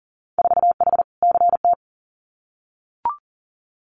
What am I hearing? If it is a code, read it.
Morse code, heard as a keyed tone, about 38 wpm, 700 Hz. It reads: 4H CT
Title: Winter Field Day Exchanges Encore 40wpm